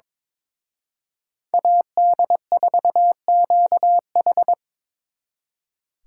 Morse code, heard as a keyed tone, about 25 wpm, 700 Hz. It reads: AD4QH
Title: Speed Racing - USA Call Signs 22wpm